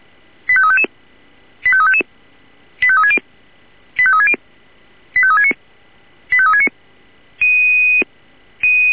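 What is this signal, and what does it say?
Five-tone selective call (ZVEI), heard as quick runs of five tones, tones of about 70 ms. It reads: 85355 85355 85360 85360 85369 85369
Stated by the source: ZVEI five-tone selective call received on an SDR.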